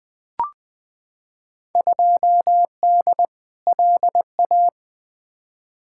Morse code, heard as a keyed tone, about 20 wpm, 700 Hz. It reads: 2D LA
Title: Field Day Exchanges - 20wpm